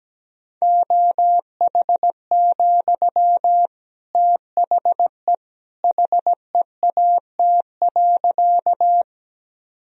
Morse code, heard as a keyed tone, about 17 wpm, 700 Hz. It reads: OH, THE HEAT.
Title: Sentences from Top 1000 Words - Review Part 3 - 17wpm